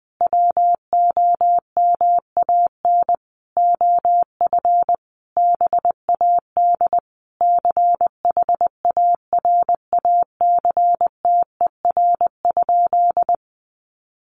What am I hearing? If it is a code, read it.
Morse code, heard as a keyed tone, about 20 wpm, 700 Hz. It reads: WOMAN OF BAD CHARACTER?
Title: Sentences from Top 1200 Words 20wpm